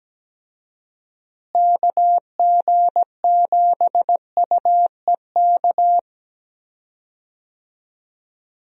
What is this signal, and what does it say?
Morse code, heard as a keyed tone, about 17 wpm, 700 Hz. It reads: KG7UEK
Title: USA Call Signs - 6 Characters - Repeated 3x - 17wpm